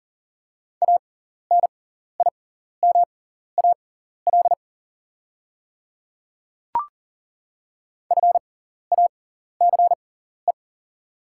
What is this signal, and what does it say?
Morse code, heard as a keyed tone, about 39 wpm, spaced as for 14 wpm, 690 Hz. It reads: ANIMAL FACE
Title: 6x Character Spacing - Top 500 Words - 40wpm